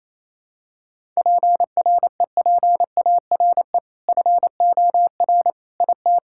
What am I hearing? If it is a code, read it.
Morse code, heard as a keyed tone, about 28 wpm, 700 Hz. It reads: PREPARE FOR IT
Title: Sentences from Top 1000 Words - Review Part 6 - 28wpm